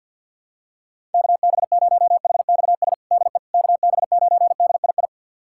Morse code, heard as a keyed tone, about 52 wpm, 690 Hz. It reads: KB0HXS DE KB9DII
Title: USA Callsign DE Callsign - 6 Characters - 50wpm